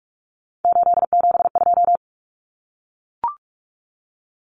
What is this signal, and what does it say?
Morse code, heard as a keyed tone, about 44 wpm, 700 Hz. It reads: 872
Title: Common Cut Numbers - 3-Digit 45wpm